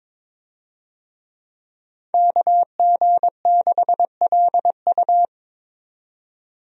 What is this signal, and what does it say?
Morse code, heard as a keyed tone, about 22 wpm, 700 Hz. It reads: KG6LU
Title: USA Call Signs - 5 Characters - 22wpm